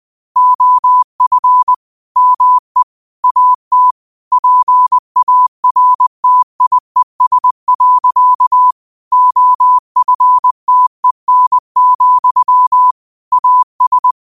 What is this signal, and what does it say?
Morse code, heard as a keyed tone, about 20 wpm, 1000 Hz. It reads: OF ME AT PARTIES. OFTEN, AS
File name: qotdfiles / 20wpm